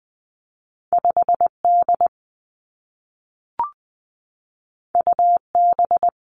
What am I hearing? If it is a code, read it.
Morse code, heard as a keyed tone, about 20 wpm, 700 Hz. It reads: 5 D U B
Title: Two Challenging Letter-Number Combinations 20wpm